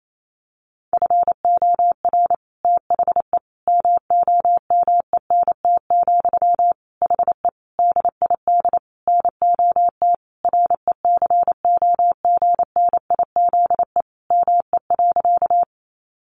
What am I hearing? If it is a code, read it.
Morse code, heard as a keyed tone, about 28 wpm, 700 Hz. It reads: FOR THE MOMENT, HE DID NOT RECOGNIZE ME.
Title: Sentences from Top 1400 Words 28wpm